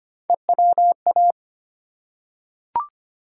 Morse code, heard as a keyed tone, about 25 wpm, 700 Hz. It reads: EWA
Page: morse-code-ninja-practice-25wpm